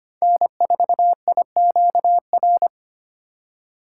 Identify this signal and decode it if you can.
Morse code, heard as a keyed tone, about 25 wpm, 700 Hz. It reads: N4IQR
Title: USA Call Signs - 5 Characters - 4x Character Spacing - 25wpm